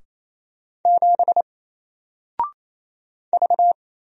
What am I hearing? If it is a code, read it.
Morse code, heard as a keyed tone, about 27 wpm, 700 Hz. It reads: 7 V
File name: Single-Letter-Number-28wpm